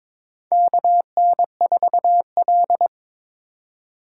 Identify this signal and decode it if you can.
Morse code, heard as a keyed tone, about 21 wpm, 700 Hz. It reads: KN4L
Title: USA Call Signs - 4x Character Spacing - 22wpm